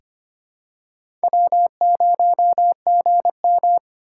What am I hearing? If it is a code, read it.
Morse code, heard as a keyed tone, about 25 wpm, 700 Hz. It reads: W 0 G M
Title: USA Call Signs - 8x Character Spacing - 25wpm